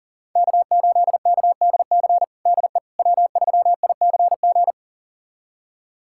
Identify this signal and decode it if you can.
Morse code, heard as a keyed tone, about 37 wpm, 700 Hz. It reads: K8KDC DE W3ICG
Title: USA Callsign DE Callsign - 5 Characters - 40wpm